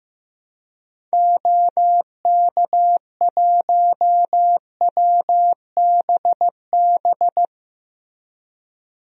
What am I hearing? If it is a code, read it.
Morse code, heard as a keyed tone, about 15 wpm, 700 Hz. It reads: OK1WBB
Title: DXCC-100 Call Signs - 6 Characters - Repeated 2x - 15wpm